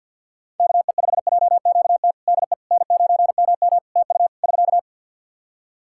Morse code, heard as a keyed tone, about 53 wpm, 700 Hz. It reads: KE5JQT DE N9MM TU <SK>
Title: K1USN SST QSO Practice 50wpm